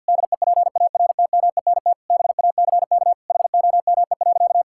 Morse code, heard as a keyed tone, about 51 wpm, 700 Hz. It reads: DEPARTMENT BACK HOME.
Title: Sentences from Top 1000 Words - Review Part 6 - 50wpm